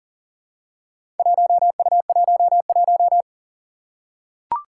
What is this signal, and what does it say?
Morse code, heard as a keyed tone, about 40 wpm, 700 Hz. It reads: 1U11
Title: Four Challenging Letter-Number Combinations 40wpm